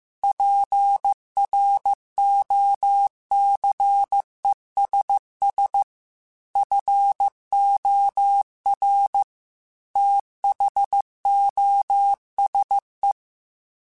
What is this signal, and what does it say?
Morse code, heard as a keyed tone, about 15 wpm, 800 Hz. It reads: PROCESS FOR THOSE
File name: Technical Subjects / Technical Subjects-mp3s / AND Gate @13